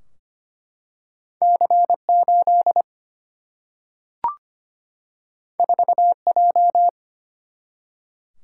Morse code, heard as a keyed tone, about 25 wpm, 700 Hz. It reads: C8 4J
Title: Single-Letter Single-Number 25wpm